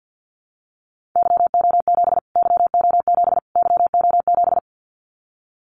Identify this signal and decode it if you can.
Morse code, heard as a keyed tone, about 50 wpm, 700 Hz. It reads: YO7 YO7 YO7
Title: Callsign Prefixes for Top 25 DXCC - Repeated 3x - 50wpm